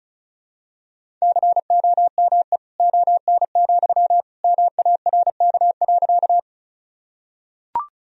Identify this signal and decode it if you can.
Morse code, heard as a keyed tone, about 36 wpm, 700 Hz. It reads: COME ON, MARK.
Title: Sentences from Top 1000 Words - Review Part 1 - 35wpm